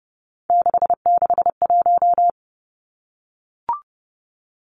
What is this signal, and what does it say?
Morse code, heard as a keyed tone, about 30 wpm, 700 Hz. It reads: 661
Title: Numbers - 3-Digit 30wpm